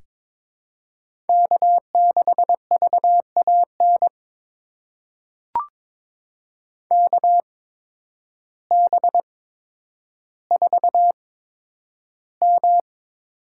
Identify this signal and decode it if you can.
Morse code, heard as a keyed tone, about 22 wpm, spaced as for 10 wpm, 700 Hz. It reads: K6VAN KB4M
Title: USA Call Signs - 8x Character Spacing - 22wpm